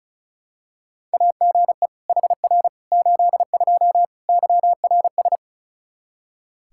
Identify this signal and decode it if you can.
Morse code, heard as a keyed tone, about 36 wpm, 700 Hz. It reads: AGE HR 82 YRS
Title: K1USN SST QSO Practice 35wpm